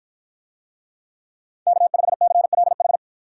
Morse code, heard as a keyed tone, about 52 wpm, 700 Hz. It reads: K5CLH